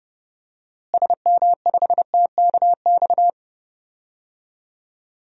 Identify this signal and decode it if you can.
Morse code, heard as a keyed tone, about 30 wpm, 700 Hz. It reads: SM5TKX
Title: DXCC-100 Call Signs - 6 Characters - Repeated 2x - 30wpm